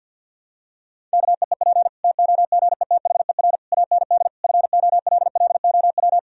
Morse code, heard as a keyed tone, about 52 wpm, 700 Hz. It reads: KEEP TOGETHER AND FOLLOW
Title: Sentences from Top 1000 Words - Review Part 1 - 50wpm